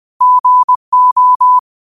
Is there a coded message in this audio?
Yes: Morse code from 0.2 to 1.6 s, heard as a keyed tone, about 20 wpm, 1000 Hz. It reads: GO